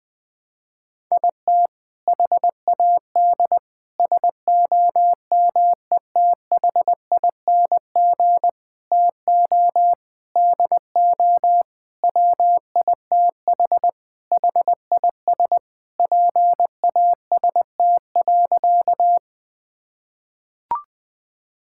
Morse code, heard as a keyed tone, about 20 wpm, 700 Hz. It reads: IT HAD SOMETHING TO DO WITH HIS PAST.
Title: Sentences with 5-Word Phrases - 20wpm